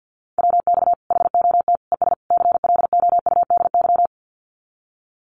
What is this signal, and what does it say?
Morse code, heard as a keyed tone, about 48 wpm, 700 Hz. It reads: WX HOT ES CLOUDY
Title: Common QSO Phrases 50wpm